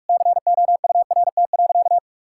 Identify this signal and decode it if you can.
Morse code, heard as a keyed tone, about 42 wpm, 700 Hz. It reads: COURT.
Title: Sentences from Top 1600 Words 45wpm